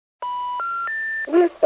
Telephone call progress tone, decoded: special information tone